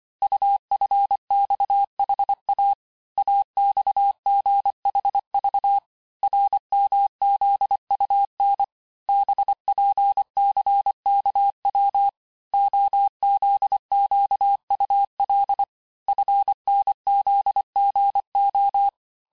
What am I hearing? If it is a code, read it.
Morse code, heard as a keyed tone, about 25 wpm, 800 Hz. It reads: UFXHA AXGHV RMZUN BPCKW OZQUL FNZGO
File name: texte-chiffre-vitesse-23mpm